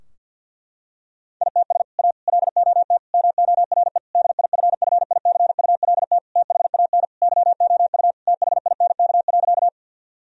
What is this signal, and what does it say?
Morse code, heard as a keyed tone, about 50 wpm, 700 Hz. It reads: ITS A LOT MORE DIFFICULT THAN YOU THINK.
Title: Sentences from Top 1000 Words - Review Part 3 - 50wpm